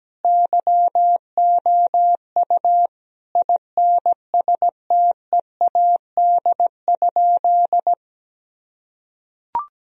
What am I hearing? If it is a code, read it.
Morse code, heard as a keyed tone, about 17 wpm, 700 Hz. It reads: YOU INSTEAD?
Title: Sentences from Top 1000 Words - Review Part 4 - 17wpm